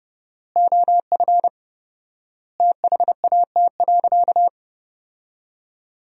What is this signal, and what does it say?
Morse code, heard as a keyed tone, about 29 wpm, 700 Hz. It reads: OF THAT.
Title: 4x Word-Spacing - Sentences from Top 300 Words - 30wpm